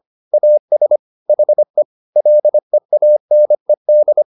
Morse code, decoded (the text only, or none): AS HE LEANED